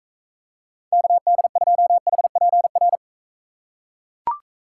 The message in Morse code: KD2HPR

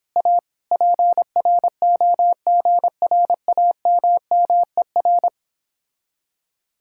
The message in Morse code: A PROGRAMMER

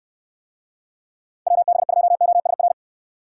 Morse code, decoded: WB2OSR

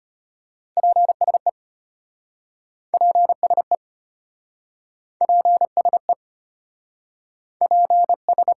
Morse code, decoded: PSE PSE PSE PS

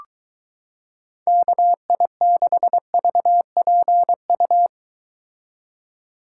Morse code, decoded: KI6VPU

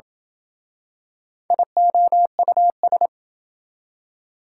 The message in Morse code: IOUS

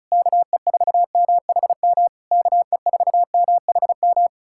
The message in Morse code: KE4MHM KE4MHM

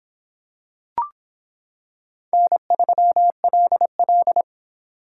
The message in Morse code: N3LL

